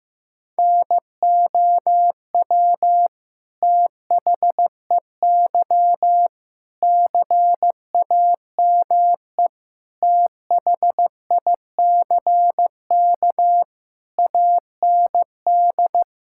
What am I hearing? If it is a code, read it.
Morse code, heard as a keyed tone, about 15 wpm, 700 Hz. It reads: NOW THEY CAME THICK AND
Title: Sentences from Top 1600 Words 15wpm